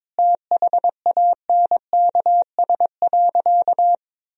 Morse code, heard as a keyed tone, about 21 wpm, 700 Hz. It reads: THANKS.
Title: Sentences from Top 1000 Words - Review Part 3 - 22wpm